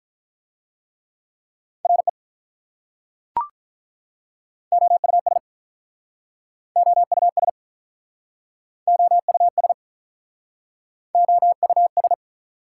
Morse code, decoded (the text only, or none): RE OUS OUS OUS OUS